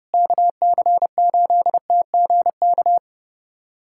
Morse code, heard as a keyed tone, about 30 wpm, 700 Hz. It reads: KC8TGK
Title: USA Callsign DE Callsign - 6 Characters - 30wpm